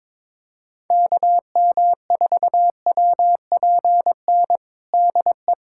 Morse code, heard as a keyed tone, about 22 wpm, 700 Hz. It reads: KM4WPN DE
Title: USA Callsign DE Callsign - 6 Characters - 22wpm